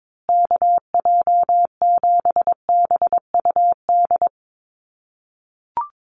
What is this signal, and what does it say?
Morse code, heard as a keyed tone, about 22 wpm, 700 Hz. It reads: KJ7BUD
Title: USA Call Signs Repeated 2x 22wpm